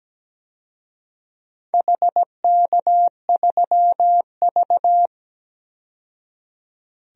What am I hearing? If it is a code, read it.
Morse code, heard as a keyed tone, about 17 wpm, 700 Hz. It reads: HK3V
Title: DXCC-100 Call Signs - 4 Characters - 17wpm